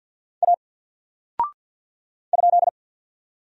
Morse code, heard as a keyed tone, about 50 wpm, 700 Hz. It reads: A ?
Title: Single Character (Letter:Number:Punctuation:Prosign) 50wpm